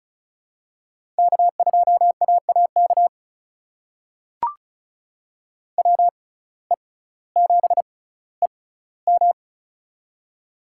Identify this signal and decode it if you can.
Morse code, heard as a keyed tone, about 36 wpm, spaced as for 18 wpm, 700 Hz. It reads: K2AAK WE7EM
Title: USA Call Signs - 5 Characters - 6x Character Spacing - 35wpm